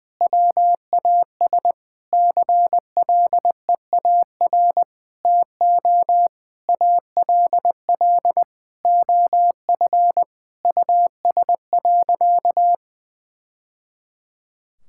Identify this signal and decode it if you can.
Morse code, heard as a keyed tone, about 20 wpm, 700 Hz. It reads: WAS CLEAR TO ALL OF US.
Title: Sentences from Top 1000 Words - Review Part 6 - 20wpm